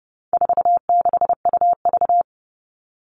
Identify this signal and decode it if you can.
Morse code, heard as a keyed tone, about 30 wpm, 700 Hz. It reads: 46UV
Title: All Cut Numbers - 4-Digit 30wpm